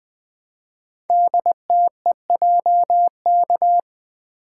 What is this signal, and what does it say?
Morse code, heard as a keyed tone, about 20 wpm, 700 Hz. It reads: DTEJK
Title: Code Groups - 5 Letters - 20wpm